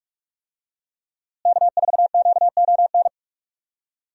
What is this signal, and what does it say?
Morse code, heard as a keyed tone, about 42 wpm, 700 Hz. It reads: K4QON
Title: USA Call Signs - 5 Characters - Repeated 3x - 45wpm